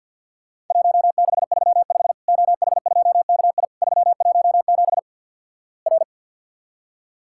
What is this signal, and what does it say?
Morse code, heard as a keyed tone, about 50 wpm, 700 Hz. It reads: 1635 OH2KI 317 R
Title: Worked All Europe DX Contest - QTC Exchange - 50wpm